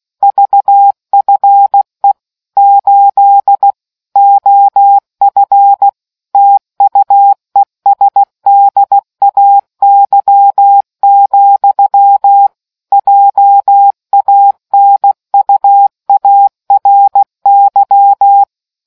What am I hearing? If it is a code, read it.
Morse code, heard as a keyed tone, about 16 wpm, 800 Hz. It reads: VFE 8 OF TUESDAY, JANUARY